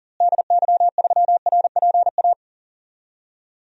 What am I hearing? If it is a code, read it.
Morse code, heard as a keyed tone, about 40 wpm, 700 Hz. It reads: DY3RPA
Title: DXCC-100 Call Signs - 6 Characters - Repeated 2x - 40wpm